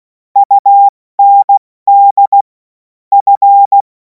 Morse code, heard as a keyed tone, about 16 wpm, 800 Hz. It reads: UND F
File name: Morsekurs / Witze / Witz 03. Februar 2026 70 F80